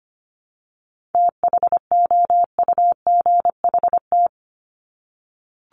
Morse code, heard as a keyed tone, about 25 wpm, 700 Hz. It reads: THOUGHT
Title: Top 500 Words 25wpm